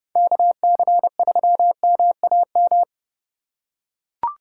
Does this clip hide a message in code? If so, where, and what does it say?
Yes: Morse code from 0.2 to 2.8 s, heard as a keyed tone, about 29 wpm, 700 Hz. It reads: KC3MAM